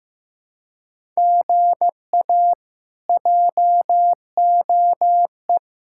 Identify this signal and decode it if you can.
Morse code, heard as a keyed tone, about 15 wpm, 700 Hz. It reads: GA JOE
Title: K1USN SST QSO Practice - 15wpm